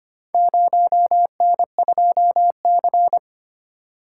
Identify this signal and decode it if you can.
Morse code, heard as a keyed tone, about 25 wpm, 700 Hz. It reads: 0N2C
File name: practice / 25wpm / Four Letter-Number Combinations - 25wpm